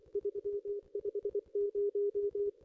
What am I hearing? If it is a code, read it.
Morse code, heard as a keyed tone, about 24 wpm, 410 Hz. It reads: 350
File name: contests / ss-cw-k5tr-op-n5rz -0100